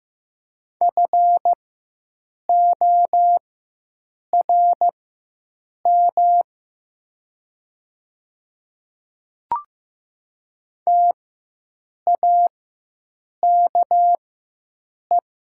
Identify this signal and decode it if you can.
Morse code, heard as a keyed tone, about 15 wpm, spaced as for 7 wpm, 700 Hz. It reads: FORM TAKE
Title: 4x Character Spacing - Top 300 Words - 15wpm